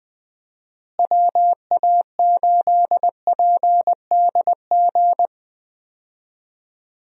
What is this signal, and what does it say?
Morse code, heard as a keyed tone, about 20 wpm, 700 Hz. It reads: WA8PDG
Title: USA Call Signs - 6 Characters - Repeated 3x - 20wpm